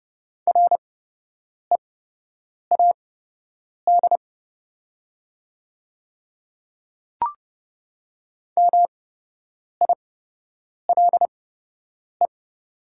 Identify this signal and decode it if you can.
Morse code, heard as a keyed tone, about 30 wpm, spaced as for 8 wpm, 700 Hz. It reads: READ MILE
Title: 8x Character Spacing - Top 300 Words - 30wpm